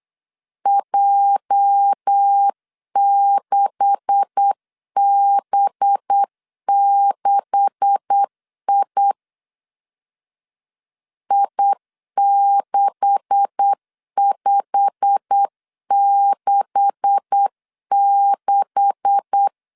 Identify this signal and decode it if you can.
Morse code, heard as a keyed tone, about 8 wpm, 800 Hz. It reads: J6B6I I6566
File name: LEZIONI / 9 286 8wpm 3425